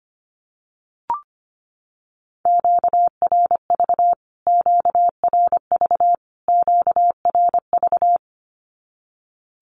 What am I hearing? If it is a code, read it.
Morse code, heard as a keyed tone, about 25 wpm, 700 Hz. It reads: QRV QRV QRV
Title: Q-Codes Repeated 3x 25wpm